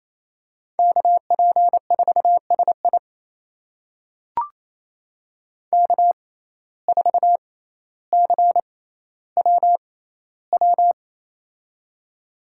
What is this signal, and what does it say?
Morse code, heard as a keyed tone, about 28 wpm, spaced as for 14 wpm, 700 Hz. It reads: KP4SI K4CWW
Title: USA Call Signs - 5 Characters - 6x Character Spacing - 28wpm